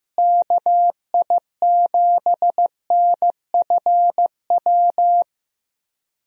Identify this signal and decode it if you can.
Morse code, heard as a keyed tone, about 15 wpm, 700 Hz. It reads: KI7NFW
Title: USA Call Signs - 6 Characters - Repeated 2x - 15wpm